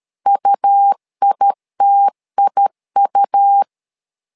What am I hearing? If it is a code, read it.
Morse code, heard as a keyed tone, about 13 wpm, 800 Hz. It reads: UITIU
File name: LEZIONI / 2 UHDN / 2 U 12wpm 1338